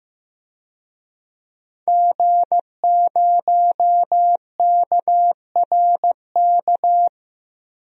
Morse code, heard as a keyed tone, about 15 wpm, 700 Hz. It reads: G0KRK
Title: DXCC-100 Call Signs - 5 Characters - Repeated 3x - 15wpm